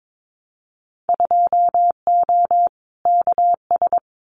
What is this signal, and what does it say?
Morse code, heard as a keyed tone, about 22 wpm, 700 Hz. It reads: 2O KS
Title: Winter Field Day Exchanges Encore 22wpm